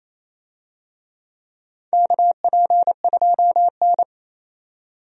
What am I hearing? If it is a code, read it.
Morse code, heard as a keyed tone, about 28 wpm, 700 Hz. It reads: KP2N
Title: DXCC-100 Call Signs - 4 Characters - 28wpm